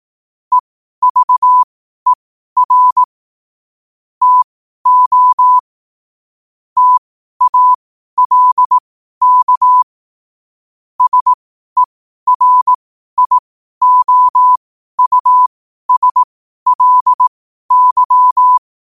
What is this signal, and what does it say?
Morse code, heard as a keyed tone, about 18 wpm, spaced as for 12 wpm, 1000 Hz. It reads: EVER TO TALK SERIOUSLY
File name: qotdfiles / 12wpm